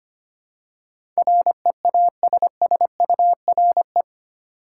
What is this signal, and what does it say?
Morse code, heard as a keyed tone, about 25 wpm, 700 Hz. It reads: REASSURE
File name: Eight Letter Words - 25wpm